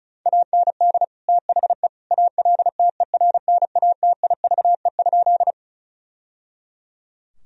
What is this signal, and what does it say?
Morse code, heard as a keyed tone, about 36 wpm, 700 Hz. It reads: AND THE ALTERNATIVE?
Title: Sentences from Top 1000 Words - Review Part 4 - 35wpm